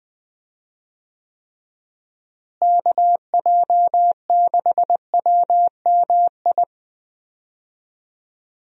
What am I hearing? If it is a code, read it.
Morse code, heard as a keyed tone, about 20 wpm, 700 Hz. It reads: KJ6WMI
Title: USA Call Signs - 6 Characters - Repeated 3x - 20wpm